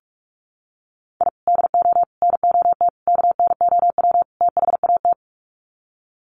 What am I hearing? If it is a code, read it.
Morse code, heard as a keyed tone, about 44 wpm, 700 Hz. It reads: I DO NOT KNOW THAT
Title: Sets of 5 Words 45wpm